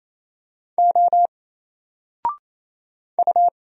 Morse code, heard as a keyed tone, about 27 wpm, 700 Hz. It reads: O U
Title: Single Character (Letter:Number:Punctuation:Prosign) 28wpm